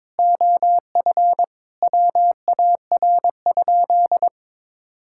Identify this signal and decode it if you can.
Morse code, heard as a keyed tone, about 22 wpm, 700 Hz. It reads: OF WAR?
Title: Sentences from Top 1000 Words - Review Part 5 - 22wpm